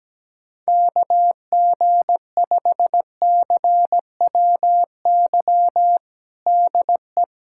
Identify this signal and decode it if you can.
Morse code, heard as a keyed tone, about 17 wpm, 700 Hz. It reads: KG5CWY DE
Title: USA Callsign DE Callsign - 6 Characters - 17wpm